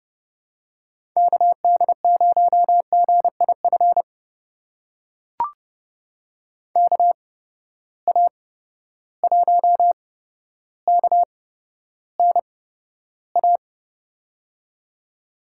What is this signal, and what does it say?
Morse code, heard as a keyed tone, about 31 wpm, spaced as for 13 wpm, 700 Hz. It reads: KD0GIF KA1KNA
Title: USA Call Signs - 6 Characters - 8x Character Spacing - 30wpm